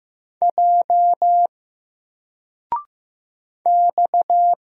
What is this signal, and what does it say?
Morse code, heard as a keyed tone, about 15 wpm, 700 Hz. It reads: J X
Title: Single Letter-Number Alt 15wpm